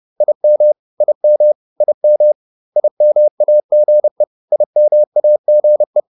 Morse code, heard as a keyed tone, about 30 wpm, 600 Hz. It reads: IM IM IM IMAGE IMAGE
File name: IM5 30